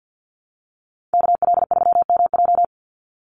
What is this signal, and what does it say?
Morse code, heard as a keyed tone, about 50 wpm, 700 Hz. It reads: KL3MJ
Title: USA Call Signs 50wpm